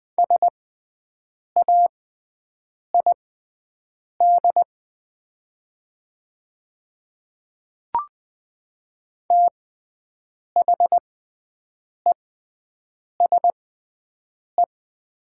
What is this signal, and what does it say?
Morse code, heard as a keyed tone, about 20 wpm, spaced as for 7 wpm, 700 Hz. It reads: SAID THESE
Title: 6x Character Spacing - Top 200 Words - 20wpm